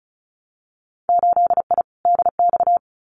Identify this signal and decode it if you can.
Morse code, heard as a keyed tone, about 36 wpm, 700 Hz. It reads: 8I DX
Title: Winter Field Day Exchanges Encore 35wpm